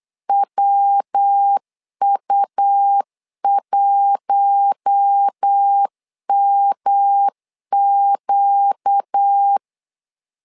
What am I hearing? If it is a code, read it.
Morse code, heard as a keyed tone, about 8 wpm, 800 Hz. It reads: WU1MQ
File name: LEZIONI / 4 Q1 8wpm 3035